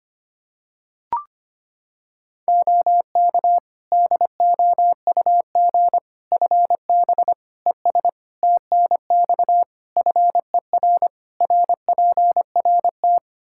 Morse code, heard as a keyed tone, about 25 wpm, 700 Hz. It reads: OK DOUG FB ES TNX FER RPRT
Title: Practice QSO - Exchange 2 - Info - 25wpm